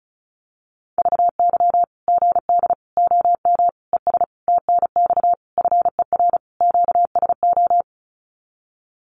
Morse code, heard as a keyed tone, about 34 wpm, 700 Hz. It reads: VY GD OM ES TNX FER QSO
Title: Common QSO Phrases 35wpm